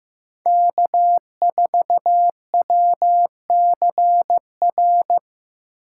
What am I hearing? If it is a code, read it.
Morse code, heard as a keyed tone, about 15 wpm, 700 Hz. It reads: K4WCR
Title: USA Call Signs - 5 Characters - 6x Character Spacing - 15wpm